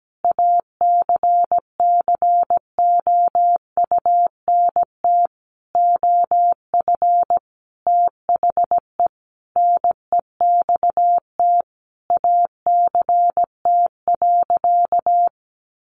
Morse code, heard as a keyed tone, about 17 wpm, 700 Hz. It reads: ACCOUNT OF THE NEXT ACT.
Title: Sentences from Top 900 Words 17wpm